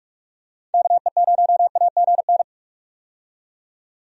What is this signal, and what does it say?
Morse code, heard as a keyed tone, about 46 wpm, 700 Hz. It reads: KE0AGN
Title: USA Call Signs - 6 Characters - 45wpm